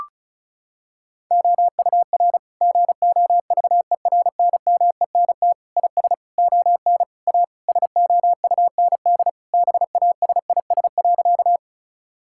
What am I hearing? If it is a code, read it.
Morse code, heard as a keyed tone, about 36 wpm, 700 Hz. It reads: OUR GOVERNMENT IS ON A SOUND BASIS.
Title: Sentences from Top 1000 Words - Review Part 5 - 35wpm